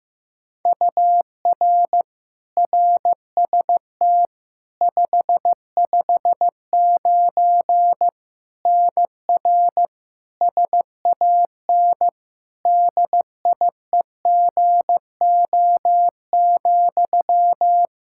Morse code, heard as a keyed tone, about 16 wpm, 700 Hz. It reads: UR RST 559 NR SAN DIEGO,
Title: Practice QSO - Exchange 1 - Beginning - 15wpm